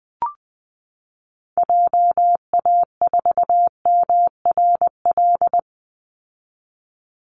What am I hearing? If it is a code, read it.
Morse code, heard as a keyed tone, about 20 wpm, 700 Hz. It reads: JA4MRL
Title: Call Signs Repeated 3x - 20wpm